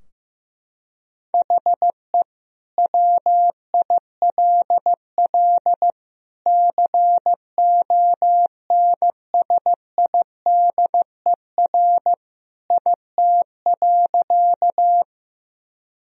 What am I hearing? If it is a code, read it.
Morse code, heard as a keyed tone, about 15 wpm, 700 Hz. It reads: HE WILL CONSIDER IT.
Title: Sentences from Top 1000 Words - Review Part 4 - 15wpm